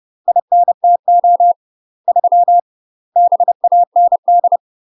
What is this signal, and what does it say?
Morse code, heard as a keyed tone, about 30 wpm, 700 Hz. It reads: INTO 3 BAND
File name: UK MASTER QSOs TRAINING / 30WPM / UK MASTER QSOs 0048